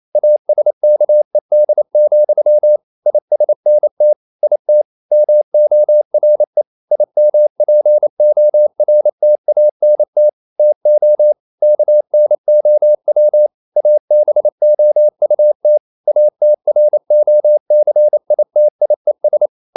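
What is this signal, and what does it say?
Morse code, heard as a keyed tone, about 28 wpm, 600 Hz. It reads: ASKED, ISNT IT MORE IMPORTANT TO KNOW ABOUT ATROCITIES
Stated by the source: Morse code practice at 28 wpm, text of Ken Follett's novel Never in English, from CW med Gnister.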